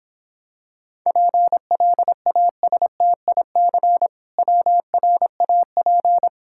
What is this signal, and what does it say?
Morse code, heard as a keyed tone, about 26 wpm, 700 Hz. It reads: PLASTIC WRAP